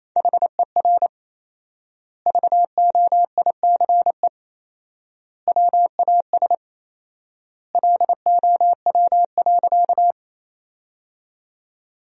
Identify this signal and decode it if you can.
Morse code, heard as a keyed tone, about 28 wpm, 700 Hz. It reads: HER VOICE WAS LOW.
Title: 4x Word-Spacing - Sentences from Top 500 Words - 28wpm